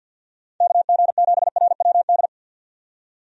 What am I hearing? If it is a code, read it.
Morse code, heard as a keyed tone, about 51 wpm, 700 Hz. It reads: KG7RWD
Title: USA Call Signs - 6 Characters - 50wpm